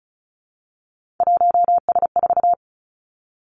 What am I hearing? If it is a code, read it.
Morse code, heard as a keyed tone, about 36 wpm, 700 Hz. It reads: 1S4
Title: Three Challenging Letter-Number Combinations 35wpm